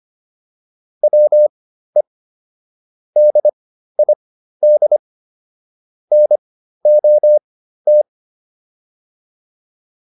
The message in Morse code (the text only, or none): WE DID NOT